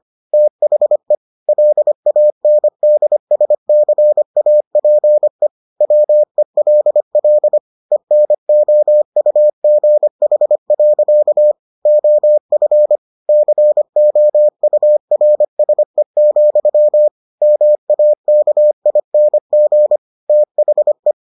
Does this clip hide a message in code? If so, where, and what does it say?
Yes: Morse code from 0.3 to 21.1 s, heard as a keyed tone, about 25 wpm, 600 Hz. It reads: THE LANDSCAPE WELL ENOUGH. OF COURSE, MAKING THE